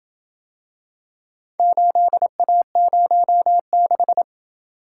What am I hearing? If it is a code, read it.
Morse code, heard as a keyed tone, about 27 wpm, 700 Hz. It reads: 8A06